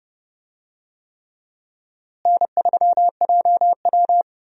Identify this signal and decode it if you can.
Morse code, heard as a keyed tone, about 30 wpm, 700 Hz. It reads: N3JW
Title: USA Call Signs - 4 Characters - Repeated 3x - 30wpm